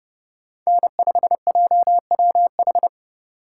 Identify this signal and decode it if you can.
Morse code, heard as a keyed tone, about 31 wpm, 700 Hz. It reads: N5JWH
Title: USA Call Signs - 5 Characters - 3x Character Spacing - 30wpm